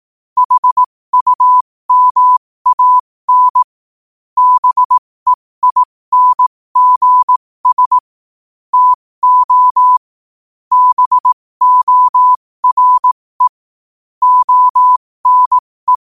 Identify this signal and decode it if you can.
Morse code, heard as a keyed tone, about 18 wpm, spaced as for 15 wpm, 1000 Hz. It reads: HUMAN BEINGS TO BORE ONE
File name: qotdfiles / 15wpm